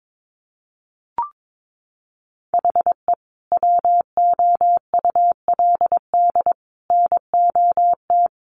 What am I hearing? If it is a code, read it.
Morse code, heard as a keyed tone, about 22 wpm, 700 Hz. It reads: HE WOULD NOT
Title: Sentences from Top 1200 Words 22wpm